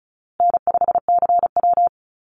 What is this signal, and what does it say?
Morse code, heard as a keyed tone, about 34 wpm, 700 Hz. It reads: N5CW
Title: Easy Call Signs Repeated 3x - 35wpm